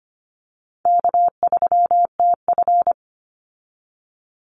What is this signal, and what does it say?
Morse code, heard as a keyed tone, about 25 wpm, 700 Hz. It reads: K3TF
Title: Easy Call Signs Repeated 2x - 25wpm